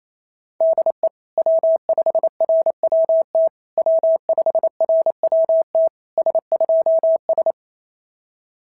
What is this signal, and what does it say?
Morse code, heard as a keyed tone, about 27 wpm, 650 Hz. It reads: DE W5RWT W5RWT S2S
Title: SOTA QSO Practice 28wpm